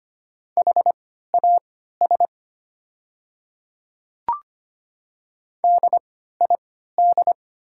Morse code, heard as a keyed tone, about 25 wpm, spaced as for 14 wpm, 700 Hz. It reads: HAS DID
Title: 3x Character Spacing - Top 100 Words - 25wpm